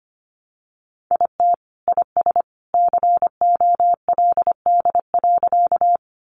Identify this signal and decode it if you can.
Morse code, heard as a keyed tone, about 25 wpm, 700 Hz. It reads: IT IS COLD.
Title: Sentences from Top 400 Words 25wpm